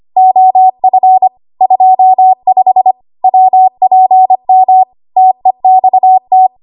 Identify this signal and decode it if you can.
Morse code, heard as a keyed tone, about 25 wpm, 750 Hz. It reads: OF 25 WPM TEXT
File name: Morse / 25WPM